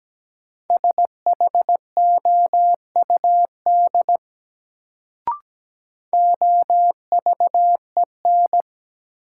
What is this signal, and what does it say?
Morse code, heard as a keyed tone, about 17 wpm, 700 Hz. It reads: SHOUD OVEN
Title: Sentences from Top 1000 Words - Review Part 4 - New Word Drill - 17wpm